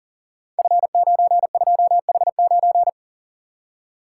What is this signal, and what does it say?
Morse code, heard as a keyed tone, about 38 wpm, 690 Hz. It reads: F92H9
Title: Code Groups - 5 Letter:Numbers - 40wpm